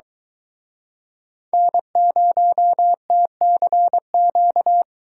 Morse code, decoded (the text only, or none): N0TCQ